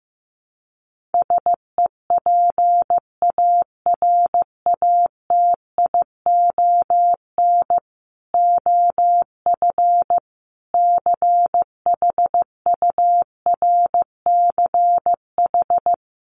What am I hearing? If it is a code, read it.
Morse code, heard as a keyed tone, about 15 wpm, 700 Hz. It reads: SEPARATION OF CHURCH
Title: Sets of 5 Words 15wpm